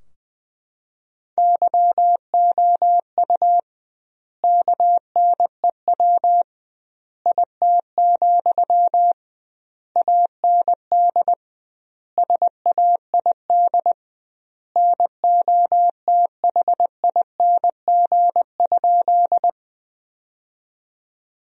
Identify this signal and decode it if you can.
Morse code, heard as a keyed tone, about 21 wpm, 700 Hz. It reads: YOU KNEW IT, AND SAID NOTHING?
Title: 2x Word-Spacing - Sentences from Top 500 Words - 20wpm